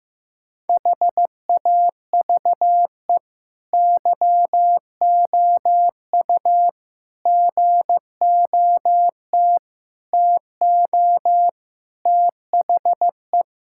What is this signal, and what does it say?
Morse code, heard as a keyed tone, about 15 wpm, 700 Hz. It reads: HAVE YOU GOT TO THE
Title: Sentences from Top 1000 Words - Review Part 6 - 15wpm